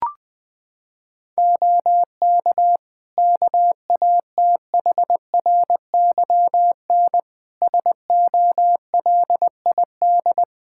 Morse code, heard as a keyed tone, about 20 wpm, 700 Hz. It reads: OK KATHRYN SOLID
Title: K1USN SST QSO Practice 20wpm